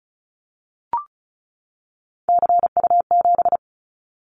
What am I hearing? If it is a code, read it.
Morse code, heard as a keyed tone, about 34 wpm, 700 Hz. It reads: CU7
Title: Callsign Prefixes for Top 100 DXCC - 35wpm